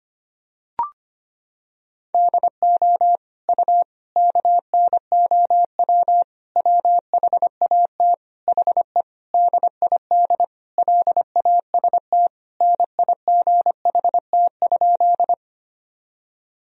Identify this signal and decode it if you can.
Morse code, heard as a keyed tone, about 25 wpm, 700 Hz. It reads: DO U KNOW WHAT HE DID LAST NIGHT?
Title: Sentences from Top 300 Words and Abbreviations 25wpm